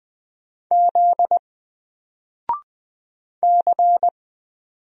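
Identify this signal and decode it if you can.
Morse code, heard as a keyed tone, about 20 wpm, 700 Hz. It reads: Z C
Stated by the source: Single letter and number Morse drill at 20 wpm, from N4LCD.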